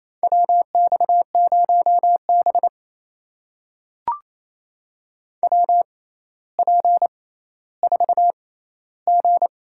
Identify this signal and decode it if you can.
Morse code, heard as a keyed tone, about 28 wpm, spaced as for 14 wpm, 700 Hz. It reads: WX0B WP4G
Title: USA Call Signs - 4 Characters - 6x Character Spacing - 28wpm